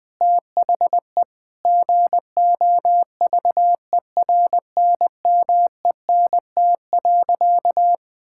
Morse code, heard as a keyed tone, about 19 wpm, 700 Hz. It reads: THE GOVERNMENT.
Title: Sentences from Top 1000 Words - Review Part 6 - 20wpm